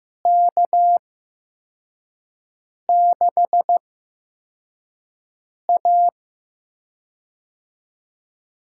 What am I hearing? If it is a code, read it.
Morse code, heard as a keyed tone, about 15 wpm, spaced as for 4 wpm, 700 Hz. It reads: K6A
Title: USA Call Signs - 4 Characters - 8x Character Spacing - 15wpm